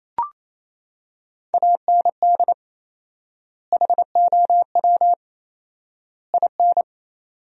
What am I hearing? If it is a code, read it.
Morse code, heard as a keyed tone, about 28 wpm, 700 Hz. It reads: AND HOW IN
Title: 4x Word-Spacing - Sentences from Top 200 Words - 28wpm